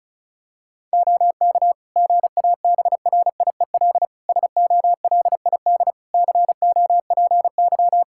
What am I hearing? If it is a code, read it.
Morse code, heard as a keyed tone, about 34 wpm, 700 Hz. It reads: OK GABRIEL SOLID COPY
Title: Practice QSO - Exchange 3 - More Info - 35wpm